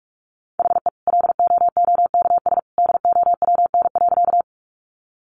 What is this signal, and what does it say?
Morse code, heard as a keyed tone, about 44 wpm, 700 Hz. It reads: HE LOOKS DOWN.
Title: Additional Easy Phrases 45wpm